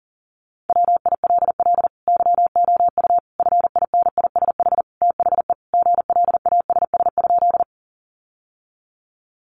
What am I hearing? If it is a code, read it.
Morse code, heard as a keyed tone, about 40 wpm, 700 Hz. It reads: WILL YOU FINISH THE GLASS?
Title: Sentences from Top 1300 Words 40wpm